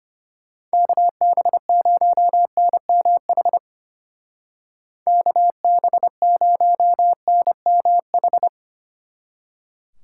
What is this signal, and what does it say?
Morse code, heard as a keyed tone, about 30 wpm, 700 Hz. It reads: KB0NMH KB0NMH
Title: Speed Racing - USA 2x3 Call Signs 25wpm